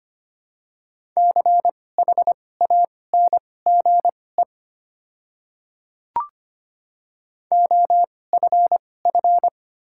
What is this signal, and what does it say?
Morse code, heard as a keyed tone, about 25 wpm, spaced as for 18 wpm, 700 Hz. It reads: CHANGE OFF
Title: 2x Character Spacing - Top 300 Words - 25wpm